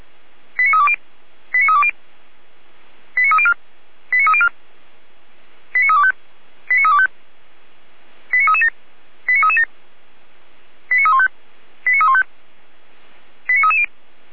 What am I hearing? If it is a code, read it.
Five-tone selective call (ZVEI), heard as quick runs of five tones, tones of about 80 ms. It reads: 89320 89320 89304 89304 89325 89325 89337 89337 89315 89315 89330